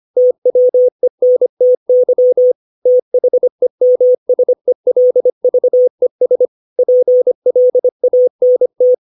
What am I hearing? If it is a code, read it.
Morse code, heard as a keyed tone, about 25 wpm, 500 Hz. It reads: TWENTY THEMSELVES PLANT
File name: 632 words / 632 Words 25WPM